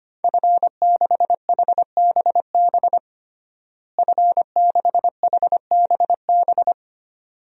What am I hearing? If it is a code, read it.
Morse code, heard as a keyed tone, about 25 wpm, 700 Hz. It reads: F6HBB F6HBB
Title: DXCC-100 Call Signs - 5 Characters - Repeated 2x - 25wpm